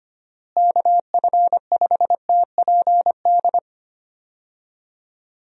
Morse code, heard as a keyed tone, about 25 wpm, 700 Hz. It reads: KF5TPD
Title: USA Call Signs - 6 Characters - Repeated 2x - 25wpm